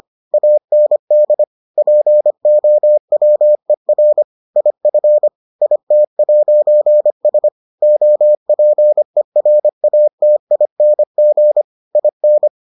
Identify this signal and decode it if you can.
Morse code, heard as a keyed tone, about 25 wpm, 600 Hz. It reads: AND POWER IF IT'S OPERATING IN